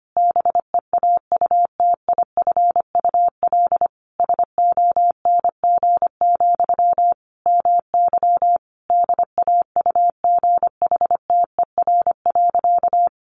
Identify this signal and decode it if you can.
Morse code, heard as a keyed tone, about 25 wpm, 700 Hz. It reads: BEAUTIFUL SONG, MY DAUGHTER.
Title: Sentences from Top 1500 Words 25wpm